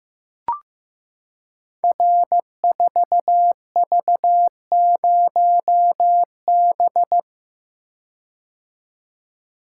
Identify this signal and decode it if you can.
Morse code, heard as a keyed tone, about 15 wpm, 700 Hz. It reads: R4V0B
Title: Code Groups - 5 Letter:Numbers - 15wpm